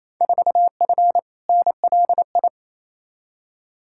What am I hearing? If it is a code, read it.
Morse code, heard as a keyed tone, about 28 wpm, 700 Hz. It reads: 4F NLI
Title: Field Day Exchanges Encore - 28wpm